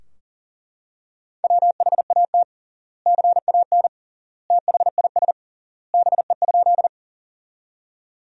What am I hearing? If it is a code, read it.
Morse code, heard as a keyed tone, about 40 wpm, 700 Hz. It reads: WHAT CAN THIS BE?
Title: 3x Word-Spacing - Sentences from Top 100 Words - 40wpm